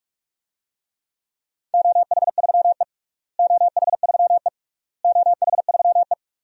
Morse code, heard as a keyed tone, about 47 wpm, 700 Hz. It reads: OH3E OH3E OH3E
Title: DXCC-100 Call Signs - 4 Characters - Repeated 3x - 45wpm